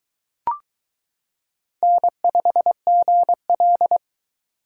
Morse code, heard as a keyed tone, about 23 wpm, 700 Hz. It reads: N5GL